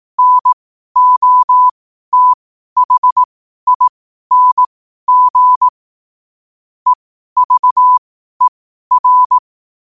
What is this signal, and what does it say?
Morse code, heard as a keyed tone, about 18 wpm, spaced as for 12 wpm, 1000 Hz. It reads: NOTHING EVER
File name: qotdfiles / 12wpm